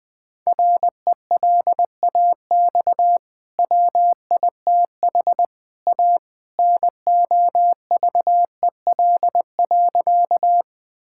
Morse code, heard as a keyed tone, about 20 wpm, 700 Hz. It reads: RELAX WITH A NOVEL.
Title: Easy Sentences with Hints - 20wpm